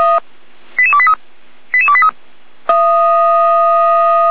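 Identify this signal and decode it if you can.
Five-tone selective call (ZVEI), heard as quick runs of five tones, tones of about 70 ms. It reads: 88283 88283
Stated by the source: Five-tone ZVEI selective call received on SDR.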